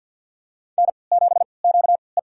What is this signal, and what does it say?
Morse code, heard as a keyed tone, about 52 wpm, spaced as for 28 wpm, 700 Hz. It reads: N7QE
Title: USA Call Signs - 3x Character Spacing - 50wpm